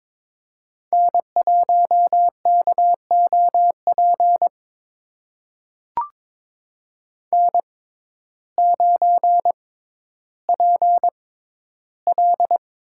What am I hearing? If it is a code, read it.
Morse code, heard as a keyed tone, about 22 wpm, spaced as for 12 wpm, 700 Hz. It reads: N1KOP N9PL
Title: USA Call Signs - 6x Character Spacing - 22wpm